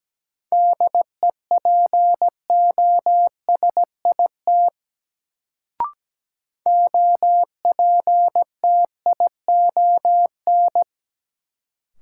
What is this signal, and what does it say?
Morse code, heard as a keyed tone, about 17 wpm, 700 Hz. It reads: DEPOSIT OPTION
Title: Sentences from Top 1300 Words - New Word Drill - 17wpm